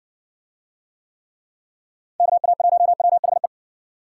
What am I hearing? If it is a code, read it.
Morse code, heard as a keyed tone, about 61 wpm, 700 Hz. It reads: KA1WHE